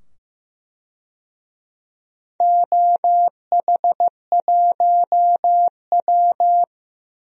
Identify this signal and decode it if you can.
Morse code, heard as a keyed tone, about 15 wpm, 700 Hz. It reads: OH1W
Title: DXCC-100 Call Signs - 4 Characters - 15wpm